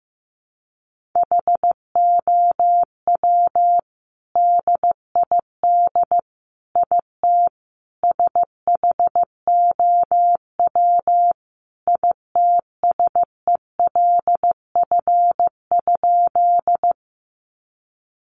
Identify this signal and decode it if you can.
Morse code, heard as a keyed tone, about 14 wpm, 700 Hz. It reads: HOW DID IT SHOW ITSELF?
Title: Sentences from Top 800 Words 15wpm